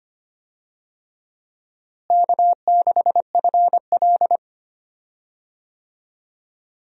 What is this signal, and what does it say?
Morse code, heard as a keyed tone, about 25 wpm, 700 Hz. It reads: K6FL
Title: USA Call Signs - 4 Characters - Repeated 3x - 25wpm